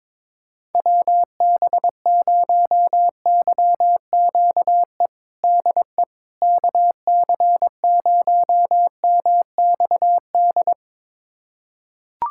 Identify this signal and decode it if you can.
Morse code, heard as a keyed tone, about 22 wpm, 700 Hz. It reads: WB0YQE DE KC0MXD
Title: USA Callsign DE Callsign - 6 Characters - 22wpm